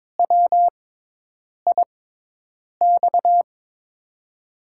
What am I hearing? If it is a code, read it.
Morse code, heard as a keyed tone, about 22 wpm, spaced as for 8 wpm, 700 Hz. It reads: WIX
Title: USA Call Signs - 6 Characters - 6x Character Spacing - 22wpm